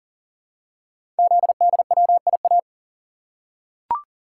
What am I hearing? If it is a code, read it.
Morse code, heard as a keyed tone, about 39 wpm, 690 Hz. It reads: ZDWIA
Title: Code Groups - 5 Letters - 40wpm